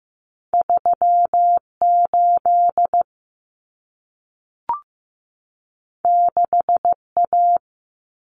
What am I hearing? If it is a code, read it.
Morse code, heard as a keyed tone, about 15 wpm, 700 Hz. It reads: 38 6A
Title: Common Cut Numbers - 2-Digit 15wpm